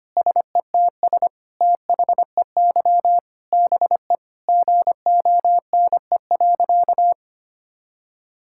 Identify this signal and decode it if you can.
Morse code, heard as a keyed tone, about 25 wpm, 700 Hz. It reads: SETS THEY BE GONE.
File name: practice / 25wpm / Sentences with 4-Word Phrases - 25wpm